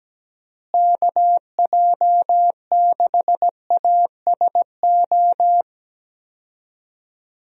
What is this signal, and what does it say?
Morse code, heard as a keyed tone, about 17 wpm, 700 Hz. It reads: KJ6ASO
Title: USA Call Signs - 6 Characters - Repeated 3x - 17wpm